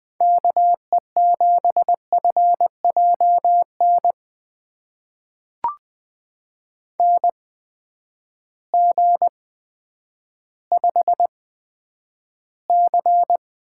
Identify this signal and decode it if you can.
Morse code, heard as a keyed tone, about 20 wpm, spaced as for 10 wpm, 700 Hz. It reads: KE7FJN NG5C
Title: USA Call Signs - 8x Character Spacing - 20wpm